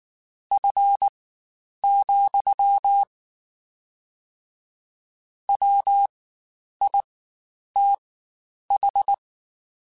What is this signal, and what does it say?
Morse code, heard as a keyed tone, about 19 wpm, spaced as for 9 wpm, 800 Hz. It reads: F, WITH
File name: Spaced Characters / Spaced Characters-mp3s / Weather 4 21 sp @18